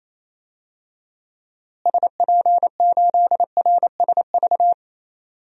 Morse code, heard as a keyed tone, about 27 wpm, 700 Hz. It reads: SP8RSV
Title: DXCC-100 Call Signs - 6 Characters - Repeated 2x - 28wpm